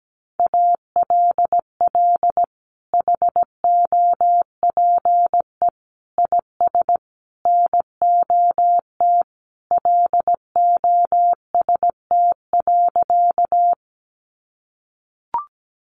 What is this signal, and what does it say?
Morse code, heard as a keyed tone, about 17 wpm, 700 Hz. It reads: ALL HOPE IS NOT LOST.
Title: Sentences from Top 1400 Words 17wpm